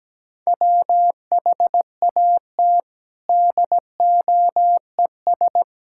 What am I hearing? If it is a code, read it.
Morse code, heard as a keyed tone, about 17 wpm, 700 Hz. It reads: WHAT DOES
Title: Sentences from Top 1000 Words - Review Part 5 - 17wpm